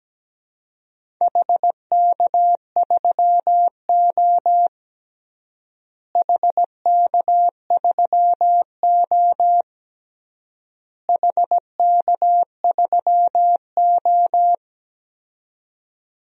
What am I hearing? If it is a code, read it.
Morse code, heard as a keyed tone, about 17 wpm, 700 Hz. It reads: HK3O HK3O HK3O
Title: DXCC-100 Call Signs - 4 Characters - Repeated 3x - 17wpm